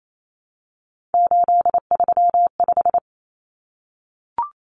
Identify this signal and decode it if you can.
Morse code, heard as a keyed tone, about 28 wpm, 700 Hz. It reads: 835
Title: Numbers - 3-Digit 28wpm